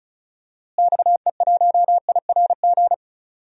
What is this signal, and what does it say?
Morse code, heard as a keyed tone, about 36 wpm, 700 Hz. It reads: XE1IRG
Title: DXCC-100 Call Signs - 6 Characters - Repeated 2x - 35wpm